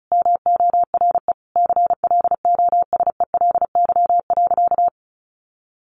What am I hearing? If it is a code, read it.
Morse code, heard as a keyed tone, about 34 wpm, 700 Hz. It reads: MORE CLOSELY.
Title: Sentences from Top 1400 Words 35wpm